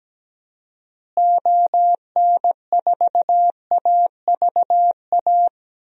Morse code, heard as a keyed tone, about 17 wpm, 700 Hz. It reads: ON4AVA
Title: DXCC-100 Call Signs - 6 Characters - Repeated 3x - 17wpm